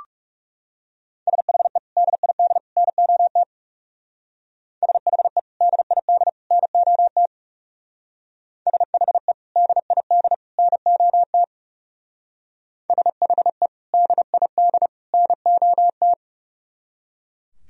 Morse code, heard as a keyed tone, about 37 wpm, 700 Hz. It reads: SHE DID NOT SHE DID NOT SHE DID NOT SHE DID NOT
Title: Speed-Racing - Sets of 3 Words 30wpm